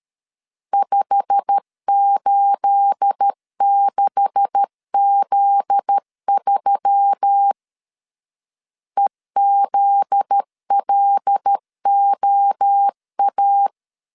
Morse code, heard as a keyed tone, about 13 wpm, 800 Hz. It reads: 586Z3 EZLOA
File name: CW / DET / 1 9 1500C S A 12wpm 3800